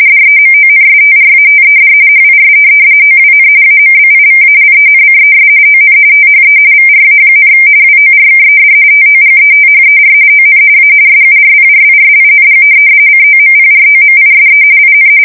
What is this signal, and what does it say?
RTTY, about 45.45 baud. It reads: DEMONSTRATED HIS OWN AMBIVALENCES ON THE VRA, WORKED TOGETHER ON A SURPRISING COMPROMISE IN
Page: 170 Shift 2125/2295